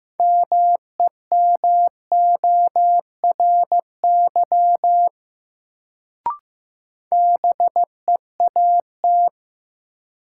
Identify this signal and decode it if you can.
Morse code, heard as a keyed tone, about 15 wpm, 700 Hz. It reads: MEMORY BEAT
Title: Sentences from Top 1300 Words - New Word Drill - 15wpm